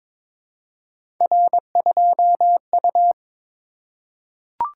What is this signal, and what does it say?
Morse code, heard as a keyed tone, about 22 wpm, 700 Hz. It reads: R2U
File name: practice / 22wpm / Three Letter-Number Combinations - 22wpm